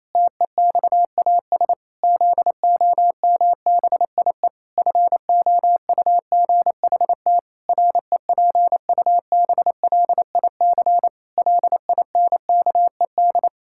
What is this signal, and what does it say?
Morse code, heard as a keyed tone, about 27 wpm, 700 Hz. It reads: TEXAS ZOMBIE FOUGHT REPUBLIC LINKED
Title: Pangram Challenge v2 - 28wpm